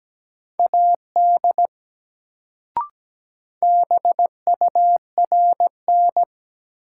Morse code, heard as a keyed tone, about 17 wpm, 700 Hz. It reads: AD BURN
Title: Sentences from Top 1300 Words - New Word Drill Alt - 17wpm